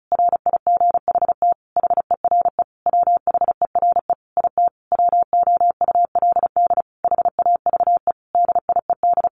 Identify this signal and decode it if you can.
Morse code, heard as a keyed tone, about 34 wpm, 700 Hz. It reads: RIGHT HERE WHERE IT WOULD HAVE DIED
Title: A Sound Of Thunder 35wpm